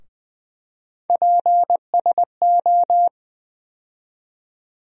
Morse code, heard as a keyed tone, about 20 wpm, 700 Hz. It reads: PSO
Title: USA Call Sign Postfixes - 3 Letters 20wpm